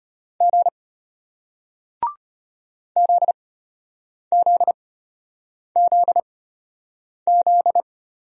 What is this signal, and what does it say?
Morse code, heard as a keyed tone, about 34 wpm, spaced as for 8 wpm, 700 Hz. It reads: G ZZZZ